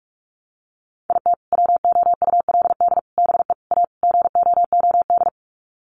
Morse code, heard as a keyed tone, about 44 wpm, 700 Hz. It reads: IT WOULD BE A GOOD
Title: Sets of 5 Words Encore2 45wpm